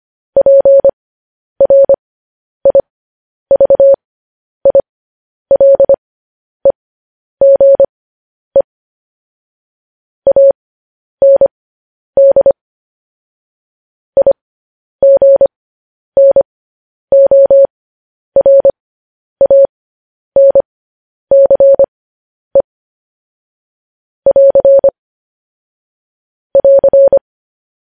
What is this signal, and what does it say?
Morse code, heard as a keyed tone, about 25 wpm, spaced as for 10 wpm, 550 Hz. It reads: PRIVILEGE AND IGNORANCE <AR> <AR>
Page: news headlines in morse code at 10 wpm